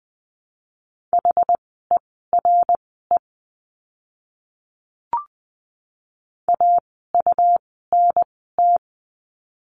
Morse code, heard as a keyed tone, about 20 wpm, spaced as for 15 wpm, 700 Hz. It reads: HERE AUNT
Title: 101 Lesson 17f - Letter u - Words 20wpm